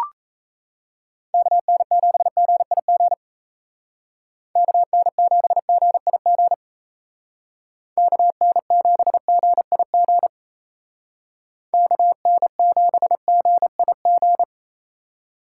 Morse code, heard as a keyed tone, about 35 wpm, 700 Hz. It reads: KN7GIG KN7GIG KN7GIG KN7GIG
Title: Speed Racing - USA 2x3 Call Signs 28wpm